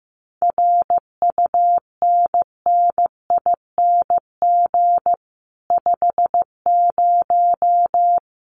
Morse code, heard as a keyed tone, about 15 wpm, 700 Hz. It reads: RUNNING 50
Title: Common QSO Phrases Encore 15wpm